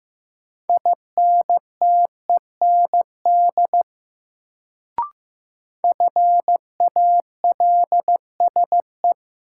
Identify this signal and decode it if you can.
Morse code, heard as a keyed tone, about 15 wpm, 700 Hz. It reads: INTEND FALSE
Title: Sentences from Top 1600 Words - New Word Drill - 15wpm